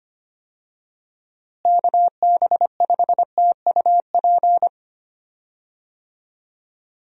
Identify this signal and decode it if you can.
Morse code, heard as a keyed tone, about 25 wpm, 700 Hz. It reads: KB5TUP
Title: USA Call Signs - 6 Characters - Repeated 2x - 25wpm